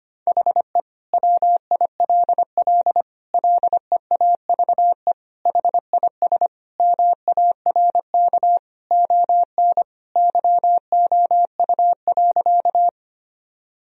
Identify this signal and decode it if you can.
Morse code, heard as a keyed tone, about 25 wpm, 700 Hz. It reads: HE WILL LEAVE HIS MARK ON YOU.
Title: Sentences from Top 1000 Words - Review Part 1 - 25wpm